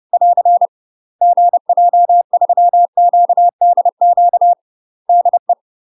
Morse code, heard as a keyed tone, about 30 wpm, 700 Hz. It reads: <AR> GJ3QDQ DE
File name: UK MASTER QSOs TRAINING / 30WPM / UK MASTER QSOs 0034